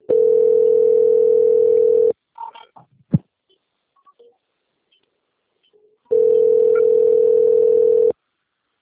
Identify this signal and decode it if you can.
Ringback tone (North America), 440 + 480 Hz.